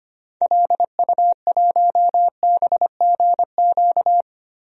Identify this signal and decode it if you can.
Morse code, heard as a keyed tone, about 25 wpm, 700 Hz. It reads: LU1BGQ
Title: DXCC-100 Call Signs - 6 Characters - 25wpm